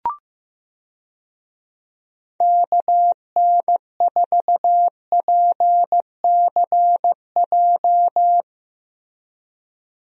Morse code, heard as a keyed tone, about 15 wpm, 700 Hz. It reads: KN4PCJ
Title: USA Call Signs - 6 Characters - Repeated 3x - 15wpm